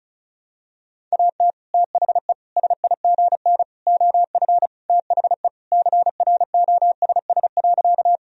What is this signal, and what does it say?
Morse code, heard as a keyed tone, about 36 wpm, 700 Hz. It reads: AT THE SIGN OF THE CROSS.
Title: Sentences from Top 1000 Words - Review Part 1 - 35wpm